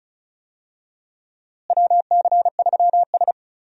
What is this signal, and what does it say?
Morse code, heard as a keyed tone, about 34 wpm, 700 Hz. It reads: WC3S
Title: USA Call Signs - 4 Characters - Repeated 3x - 35wpm